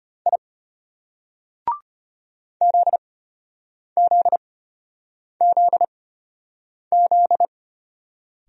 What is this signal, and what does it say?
Morse code, heard as a keyed tone, about 32 wpm, spaced as for 8 wpm, 700 Hz. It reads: I ZZZZ